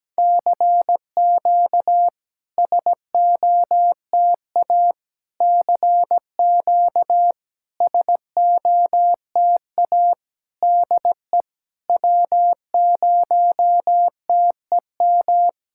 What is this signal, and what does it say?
Morse code, heard as a keyed tone, about 17 wpm, 700 Hz. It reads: CQ SOTA CQ SOTA DE W0TEM
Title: SOTA QSO Practice 17wpm